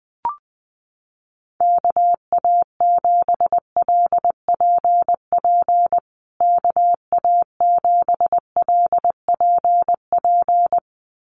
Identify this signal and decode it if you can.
Morse code, heard as a keyed tone, about 20 wpm, 700 Hz. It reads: KA7LPP KA7LPP
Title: USA Call Signs Repeated 2x 20wpm